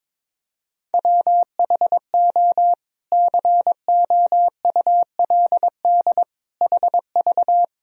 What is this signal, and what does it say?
Morse code, heard as a keyed tone, about 21 wpm, 700 Hz. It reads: WHO COULD HV
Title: Sentences from Top 300 Words and Abbreviations 22wpm